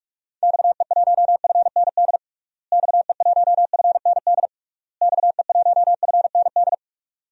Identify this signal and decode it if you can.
Morse code, heard as a keyed tone, about 46 wpm, 700 Hz. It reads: XE1FND XE1FND XE1FND
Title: DXCC-100 Call Signs - 6 Characters - Repeated 3x - 45wpm